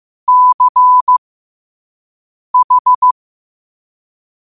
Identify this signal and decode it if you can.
Morse code, heard as a keyed tone, about 15 wpm, spaced as for 5 wpm, 1000 Hz. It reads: CH
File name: qotdfiles / 5wpm